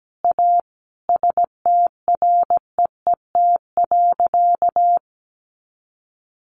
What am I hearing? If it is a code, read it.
Morse code, heard as a keyed tone, about 17 wpm, 700 Hz. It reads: A STREET.
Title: Sentences from Top 500 Words 17wpm